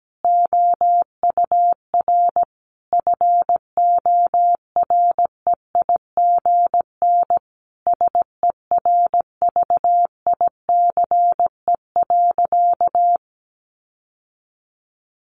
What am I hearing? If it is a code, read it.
Morse code, heard as a keyed tone, about 17 wpm, 700 Hz. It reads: OUR FOREIGN SERVICE.
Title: Sentences from Top 1400 Words 17wpm